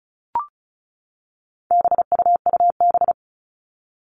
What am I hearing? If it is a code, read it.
Morse code, heard as a keyed tone, about 34 wpm, 700 Hz. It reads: BUUB
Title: All Cut Numbers - 4-Digit 35wpm